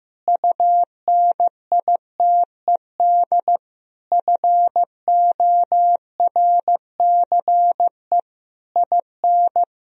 Morse code, heard as a keyed tone, about 16 wpm, 700 Hz. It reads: UNITED FORCE IN
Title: Sentences from Top 1000 Words - Review Part 4 - 15wpm